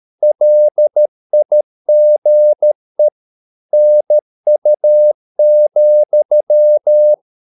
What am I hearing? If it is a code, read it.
Morse code, heard as a keyed tone, about 13 wpm, 600 Hz. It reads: LIGE NU,
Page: Ferie 13 wpm